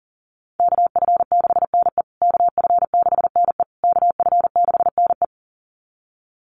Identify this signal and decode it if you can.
Morse code, heard as a keyed tone, about 40 wpm, 700 Hz. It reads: KF6NE KF6NE KF6NE
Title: USA Call Signs Repeated 3x 40wpm